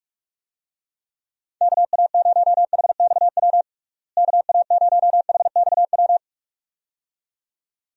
Morse code, heard as a keyed tone, about 46 wpm, 700 Hz. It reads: KA0HXW KA0HXW
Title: USA Call Signs - 6 Characters - Repeated 2x - 45wpm